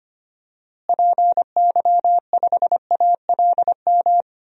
Morse code, heard as a keyed tone, about 25 wpm, 700 Hz. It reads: PY5ALM
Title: DXCC-100 Call Signs - 6 Characters - Repeated 3x - 25wpm